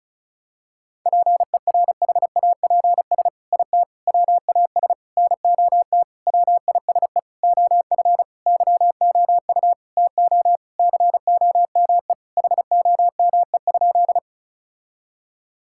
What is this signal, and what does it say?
Morse code, heard as a keyed tone, about 34 wpm, 700 Hz. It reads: PERHAPS IT WAS NOT WISE OF YOU TO COME HOME?
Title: Sentences from Top 1600 Words 35wpm